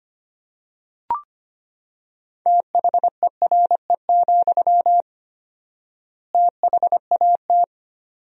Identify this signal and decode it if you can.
Morse code, heard as a keyed tone, about 25 wpm, 700 Hz. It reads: THERE, THAT
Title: 4x Word-Spacing - Sentences from Top 200 Words - 25wpm